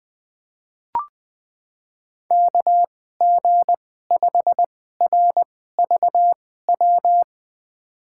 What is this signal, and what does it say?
Morse code, heard as a keyed tone, about 20 wpm, spaced as for 14 wpm, 700 Hz. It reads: KG5RVW
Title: USA Call Signs - 2x Character Spacing - 20wpm